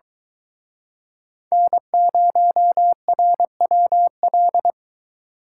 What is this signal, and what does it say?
Morse code, heard as a keyed tone, about 23 wpm, 700 Hz. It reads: N0RWL